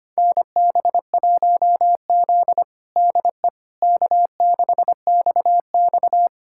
Morse code, heard as a keyed tone, about 25 wpm, 700 Hz. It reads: NB1Z DE K6XX
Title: USA Callsign DE Callsign - 4 Characters - 25wpm